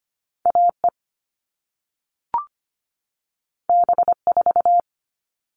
Morse code, heard as a keyed tone, about 25 wpm, 700 Hz. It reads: AE B4
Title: All Cut Numbers - 2-Digit 25wpm